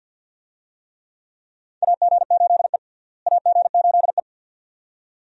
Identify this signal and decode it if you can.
Morse code, heard as a keyed tone, about 47 wpm, 700 Hz. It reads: AG8E AG8E
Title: USA Call Signs - 4 Characters - Repeated 2x - 50wpm